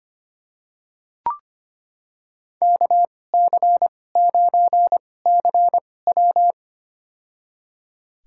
Morse code, heard as a keyed tone, about 25 wpm, spaced as for 18 wpm, 700 Hz. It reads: KC9CW
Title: USA Call Signs - 2x Character Spacing - 25wpm